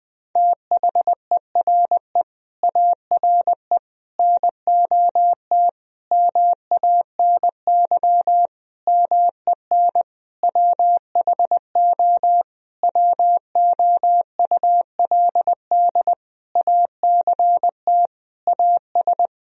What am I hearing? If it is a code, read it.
Morse code, heard as a keyed tone, about 20 wpm, 700 Hz. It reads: THERE ARE NOT MANY MEN WHO WOULD ACT AS
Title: Sentences from Top 1000 Words - Review Part 1 - 20wpm